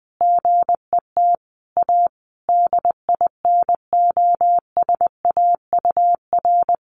Morse code, heard as a keyed tone, about 20 wpm, 700 Hz. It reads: GET A DINOSAUR
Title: A Sound Of Thunder 20wpm